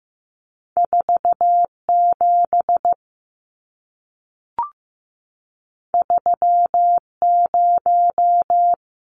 Morse code, heard as a keyed tone, about 15 wpm, 700 Hz. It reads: 4 7 3 0
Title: Numbers - 2-Digit 15wpm